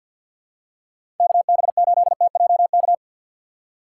Morse code, heard as a keyed tone, about 53 wpm, 700 Hz. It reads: KB8TJK
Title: USA Call Signs - 2x Character Spacing - 50wpm